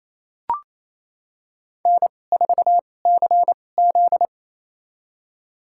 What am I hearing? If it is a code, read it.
Morse code, heard as a keyed tone, about 28 wpm, spaced as for 20 wpm, 700 Hz. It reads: N4CZ
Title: USA Call Signs - 4 Characters - 2x Character Spacing - 28wpm